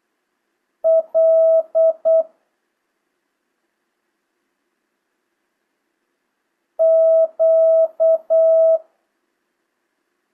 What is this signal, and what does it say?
Morse code, heard as a keyed tone, about 8 wpm, 650 Hz. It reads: L Q
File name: jh8cbh / 2025cw